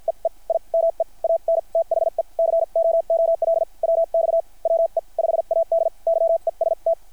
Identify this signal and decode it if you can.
Morse code, heard as a keyed tone, about 50 wpm, spaced as for 41 wpm, 660 Hz. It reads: EE I MEANTHE COOL WX WE 5AD YEST
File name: archived / CRSnet